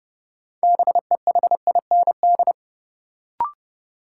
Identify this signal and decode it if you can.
Morse code, heard as a keyed tone, about 30 wpm, 700 Hz. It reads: BEHIND
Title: Sentences from Top 500 Words - New Word Drill - 30wpm